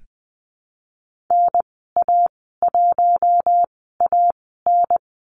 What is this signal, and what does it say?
Morse code, heard as a keyed tone, about 20 wpm, spaced as for 14 wpm, 700 Hz. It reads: NA1AN
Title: 042 Lesson 8g - Number 1 - Call Signs 20wpm